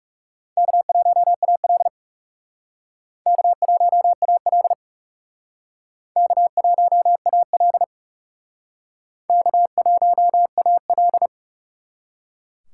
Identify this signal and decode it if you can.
Morse code, heard as a keyed tone, about 37 wpm, 700 Hz. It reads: K1AL K1AL K1AL K1AL
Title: Speed Racing - USA 1x2 Call Signs 30wpm